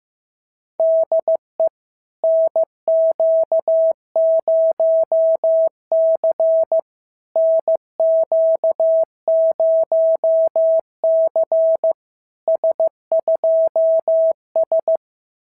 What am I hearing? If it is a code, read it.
Morse code, heard as a keyed tone, about 15 wpm, 650 Hz. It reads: DE NQ0C NQ0C S2S
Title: SOTA QSO Practice 15wpm